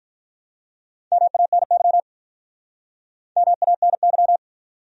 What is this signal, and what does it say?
Morse code, heard as a keyed tone, about 43 wpm, 700 Hz. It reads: MANY MANY
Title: Speed-Racing Top 100 Words 35wpm